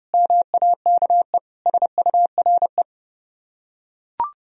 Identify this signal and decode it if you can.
Morse code, heard as a keyed tone, about 28 wpm, 700 Hz. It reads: MAKE SURE